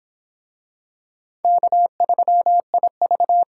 Morse code, heard as a keyed tone, about 26 wpm, 700 Hz. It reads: K3IV